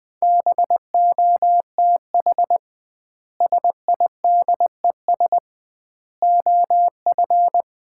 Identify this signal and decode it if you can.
Morse code, heard as a keyed tone, about 20 wpm, 700 Hz. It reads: BOTH SIDES OF
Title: 2x Word-Spacing - Sets of 4 Words - 20wpm